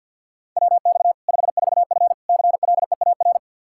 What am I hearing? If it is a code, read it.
Morse code, heard as a keyed tone, about 49 wpm, 700 Hz. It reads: WX 54F CLEAR
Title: K1USN SST QSO Practice 50wpm